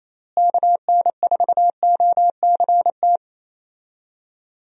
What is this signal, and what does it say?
Morse code, heard as a keyed tone, about 28 wpm, 700 Hz. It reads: KN4OCT
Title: USA Call Signs - 6 Characters - Repeated 3x - 28wpm